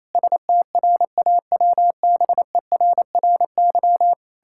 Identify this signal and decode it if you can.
Morse code, heard as a keyed tone, about 28 wpm, 700 Hz. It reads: STRAWBERRY
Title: Pangram Challenge v2 - 28wpm